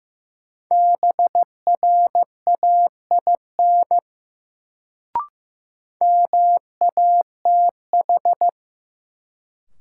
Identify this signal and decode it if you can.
Morse code, heard as a keyed tone, about 16 wpm, 700 Hz. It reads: BRAIN MATH
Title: Sentences from Top 1000 Words - Review Part 6 - New Word Drill - 15wpm